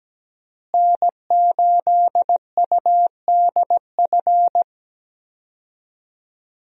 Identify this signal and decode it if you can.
Morse code, heard as a keyed tone, about 17 wpm, 700 Hz. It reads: N8UDF
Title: USA Call Signs - 5 Characters - Repeated 2x - 17wpm